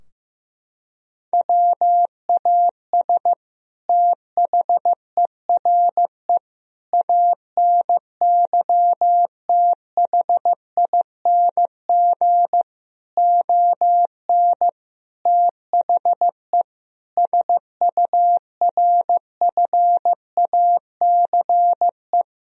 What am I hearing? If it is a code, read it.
Morse code, heard as a keyed tone, about 15 wpm, 700 Hz. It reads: WAS THERE ANYTHING ON THE SURFACE
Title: Sentences from Top 1000 Words - Review Part 3 - 15wpm